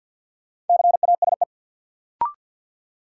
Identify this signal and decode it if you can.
Morse code, heard as a keyed tone, about 53 wpm, 700 Hz. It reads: CASE
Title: Sentences from Top 700 Words - New Word Drill - 50wpm